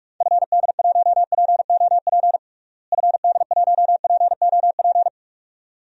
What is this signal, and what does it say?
Morse code, heard as a keyed tone, about 46 wpm, 700 Hz. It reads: FD1POP FD1POP
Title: DXCC-100 Call Signs - 6 Characters - Repeated 2x - 45wpm